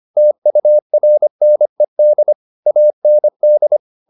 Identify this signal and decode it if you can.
Morse code, heard as a keyed tone, about 25 wpm, 600 Hz. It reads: TURNED AND
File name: the old man and the sea part 1 25wpm